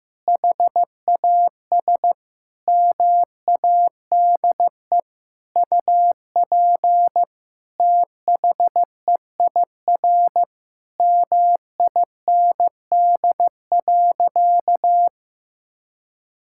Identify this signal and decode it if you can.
Morse code, heard as a keyed tone, about 14 wpm, 700 Hz. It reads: HAS MADE UP THEIR MIND.
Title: Sentences from Top 1000 Words - Review Part 5 - 15wpm